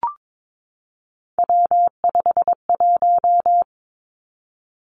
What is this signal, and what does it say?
Morse code, heard as a keyed tone, about 22 wpm, 700 Hz. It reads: W51
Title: Three Challenging Letter-Number Combinations 22wpm